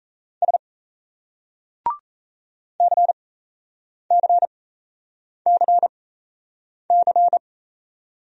Morse code, heard as a keyed tone, about 35 wpm, spaced as for 8 wpm, 700 Hz. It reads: S CCCC